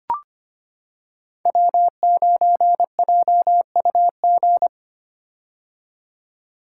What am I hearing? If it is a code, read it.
Morse code, heard as a keyed tone, about 25 wpm, 700 Hz. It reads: W9JUG
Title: K1USN SST QSO Practice - 25wpm